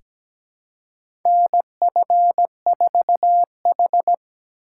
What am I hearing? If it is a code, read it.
Morse code, heard as a keyed tone, about 17 wpm, 700 Hz. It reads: NF4H
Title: USA Call Signs - 4 Characters - 2x Character Spacing - 17wpm